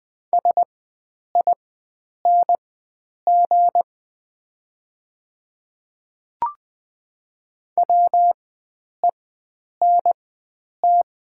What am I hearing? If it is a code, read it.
Morse code, heard as a keyed tone, about 20 wpm, spaced as for 9 wpm, 700 Hz. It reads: SING WENT
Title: 4x Character Spacing - Top 500 Words - 20wpm